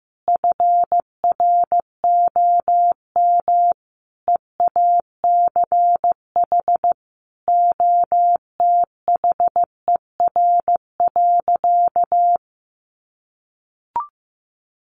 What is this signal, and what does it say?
Morse code, heard as a keyed tone, about 15 wpm, 700 Hz. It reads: FROM EACH OTHER.
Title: Easy Phrases 15wpm